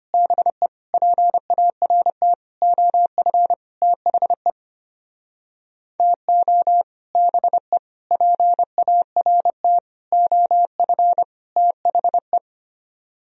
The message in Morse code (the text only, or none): BE PART OF THE TO BE PART OF THE